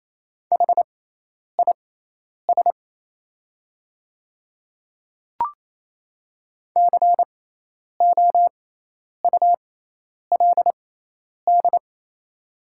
Morse code, heard as a keyed tone, about 28 wpm, spaced as for 10 wpm, 700 Hz. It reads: HIS COULD
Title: 6x Character Spacing - Top 100 Words - 28wpm